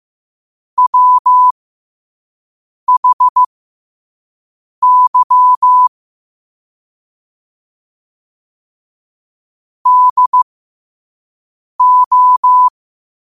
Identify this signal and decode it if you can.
Morse code, heard as a keyed tone, about 15 wpm, spaced as for 5 wpm, 1000 Hz. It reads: WHY DO
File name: qotdfiles / 5wpm